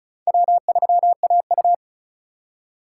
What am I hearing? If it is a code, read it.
Morse code, heard as a keyed tone, about 36 wpm, 700 Hz. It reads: W3AU
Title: USA Call Signs - 4 Characters - 4x Character Spacing - 35wpm